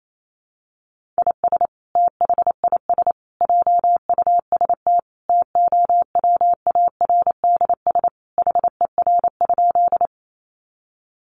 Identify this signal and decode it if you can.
Morse code, heard as a keyed tone, about 28 wpm, 700 Hz. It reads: IS THIS JUST TOWARDS HER?
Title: Sentences from Top 1100 Words 28wpm